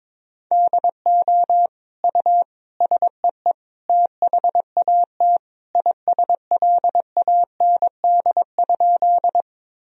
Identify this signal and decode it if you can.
Morse code, heard as a keyed tone, about 22 wpm, 700 Hz. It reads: DO U SEE THAT ISLAND?
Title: Sentences from Top 500 Words and Abbreviations 22wpm